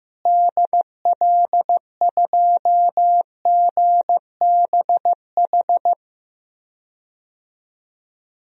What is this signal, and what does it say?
Morse code, heard as a keyed tone, about 15 wpm, 700 Hz. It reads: DL2GBH
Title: DXCC-100 Call Signs - 6 Characters - Repeated 3x - 15wpm